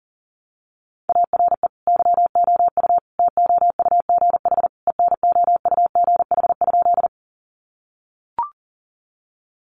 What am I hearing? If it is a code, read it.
Morse code, heard as a keyed tone, about 40 wpm, 700 Hz. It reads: ARE YOU TOUGH ENOUGH?
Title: Sentences from Top 1200 Words 40wpm